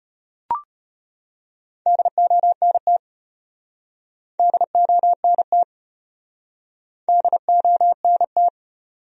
Morse code, heard as a keyed tone, about 36 wpm, 700 Hz. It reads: DONT DONT DONT